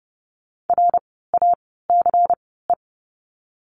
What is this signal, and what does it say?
Morse code, heard as a keyed tone, about 30 wpm, spaced as for 17 wpm, 700 Hz. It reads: RACE
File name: Lesson 16f - Letter c - Words 30wpm